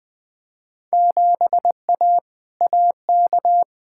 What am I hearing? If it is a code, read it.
Morse code, heard as a keyed tone, about 21 wpm, 700 Hz. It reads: 7A AK
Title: Field Day Exchanges Encore - 20wpm